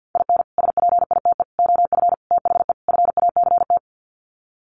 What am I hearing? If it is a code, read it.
Morse code, heard as a keyed tone, about 50 wpm, 700 Hz. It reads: IN SPITE OF THE FACT
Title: Sets of 5 Words Encore2 50wpm